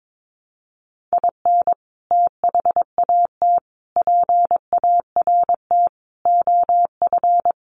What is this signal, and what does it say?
Morse code, heard as a keyed tone, about 22 wpm, 700 Hz. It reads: IN THAT PART OF
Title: Sets of 5 Words Encore2 22wpm